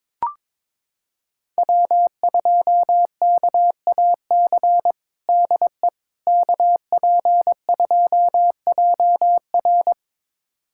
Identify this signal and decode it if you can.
Morse code, heard as a keyed tone, about 22 wpm, 700 Hz. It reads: W2KAC DE KP2JR
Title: USA Callsign DE Callsign - 5 Characters - 22wpm